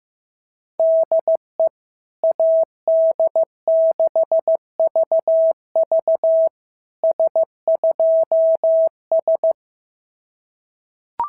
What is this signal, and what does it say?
Morse code, heard as a keyed tone, about 14 wpm, 650 Hz. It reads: DE AD6VV S2S
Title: SOTA QSO Practice 15wpm